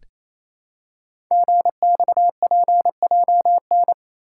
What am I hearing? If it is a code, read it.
Morse code, heard as a keyed tone, about 27 wpm, 700 Hz. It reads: GXPJN
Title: Code Groups - 5 Letter:Numbers - 28wpm